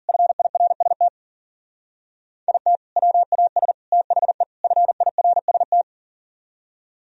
Morse code, heard as a keyed tone, about 50 wpm, 700 Hz. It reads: FIRST IT WAS THE FIRST